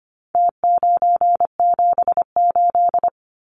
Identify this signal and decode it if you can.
Morse code, heard as a keyed tone, about 25 wpm, 700 Hz. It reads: T978
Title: Numbers - Contest Style - 4-Digit 25wpm